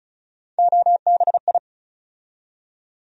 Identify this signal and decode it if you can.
Morse code, heard as a keyed tone, about 34 wpm, 700 Hz. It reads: OBI
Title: USA Call Sign Postfixes - 3 Letters 35wpm